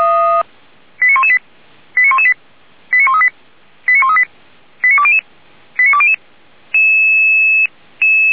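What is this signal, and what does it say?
Five-tone selective call (ZVEI), heard as quick runs of five tones, tones of about 70 ms. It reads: 89118 89118 89138 89138 89200 89200
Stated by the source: ZVEI five-tone selective call 89227 received on SDR, from the Feuerwehr Orlamünde fire brigade.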